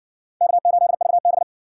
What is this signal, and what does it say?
Morse code, heard as a keyed tone, about 60 wpm, 700 Hz. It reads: K8VB